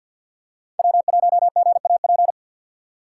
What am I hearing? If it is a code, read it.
Morse code, heard as a keyed tone, about 49 wpm, 700 Hz. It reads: W1GAP
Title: USA Call Signs - 5 Characters - 3x Character Spacing - 50wpm